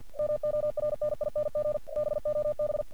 Morse code, heard as a keyed tone, about 46 wpm, 600 Hz. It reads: MORNING BOB
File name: archived / CRSnet